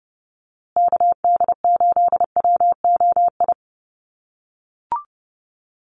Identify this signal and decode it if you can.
Morse code, heard as a keyed tone, about 28 wpm, 700 Hz. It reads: KD8WOI
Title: USA Call Signs Repeated 2x 30wpm